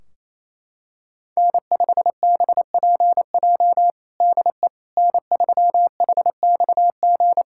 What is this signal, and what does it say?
Morse code, heard as a keyed tone, about 28 wpm, 700 Hz. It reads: N5BPJ DE N3HXG
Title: K1USN SST QSO Practice 28wpm